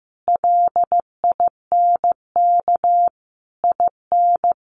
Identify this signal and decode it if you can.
Morse code, heard as a keyed tone, about 15 wpm, 700 Hz. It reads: LINK IN
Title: Sentences from Top 1500 Words 15wpm